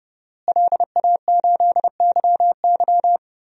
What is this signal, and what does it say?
Morse code, heard as a keyed tone, about 32 wpm, 700 Hz. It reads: LA8YY
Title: DXCC-100 Call Signs - 5 Characters - Repeated 3x - 30wpm